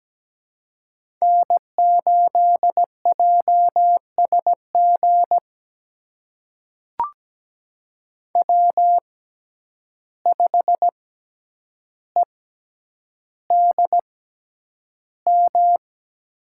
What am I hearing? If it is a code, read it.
Morse code, heard as a keyed tone, about 17 wpm, spaced as for 9 wpm, 700 Hz. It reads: N8JSG W5EDM
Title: USA Call Signs - 5 Characters - 6x Character Spacing - 17wpm